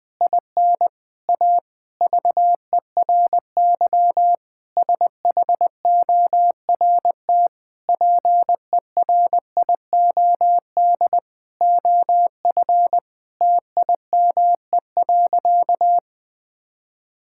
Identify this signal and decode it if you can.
Morse code, heard as a keyed tone, about 21 wpm, 700 Hz. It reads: IN A VERY SHORT PERIOD OF TIME.
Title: Sentences with 7-Word Phrases - 20wpm